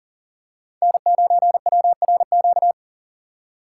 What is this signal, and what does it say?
Morse code, heard as a keyed tone, about 41 wpm, 690 Hz. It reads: N9WRQ
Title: USA Call Signs - 5 Characters - 6x Character Spacing - 40wpm